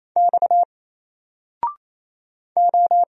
Morse code, heard as a keyed tone, about 27 wpm, 700 Hz. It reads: X O
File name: Single-Letter-Number-28wpm